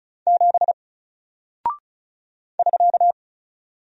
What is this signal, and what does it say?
Morse code, heard as a keyed tone, about 36 wpm, 700 Hz. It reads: 7 <SK>
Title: Single Character (Letter:Number:Punctuation:Prosign) 35wpm